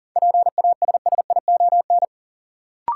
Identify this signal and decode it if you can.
Morse code, heard as a keyed tone, about 38 wpm, 700 Hz. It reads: PASSION
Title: Sentences from Top 1300 Words - New Word Drill - 40wpm